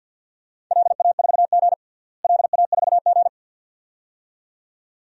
Morse code, heard as a keyed tone, about 52 wpm, 700 Hz. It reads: LA4G LA4G
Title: DXCC-100 Call Signs - 4 Characters - Repeated 2x - 50wpm